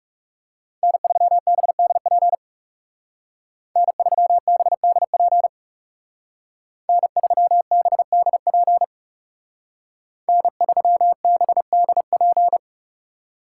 Morse code, heard as a keyed tone, about 35 wpm, 700 Hz. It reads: N3BDP N3BDP N3BDP N3BDP
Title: Speed Racing - USA 1x3 Call Signs 30wpm